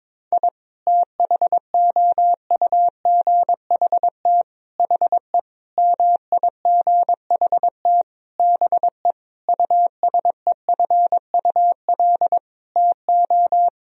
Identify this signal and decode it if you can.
Morse code, heard as a keyed tone, about 22 wpm, 700 Hz. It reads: I THOUGHT HE MIGHT BE USEFUL TO
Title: Sentences from Top 1000 Words - Review Part 4 - 22wpm